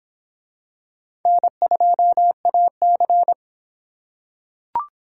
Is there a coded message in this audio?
Yes: Morse code from 1.3 to 3.3 s, heard as a keyed tone, about 25 wpm, 700 Hz. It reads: N2AC